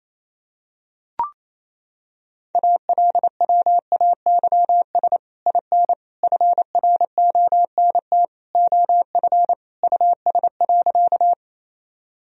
Morse code, heard as a keyed tone, about 28 wpm, 700 Hz. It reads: ALWAYS IN FRONT OF US.
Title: Sentences from Top 1000 Words - Review Part 2 - 28wpm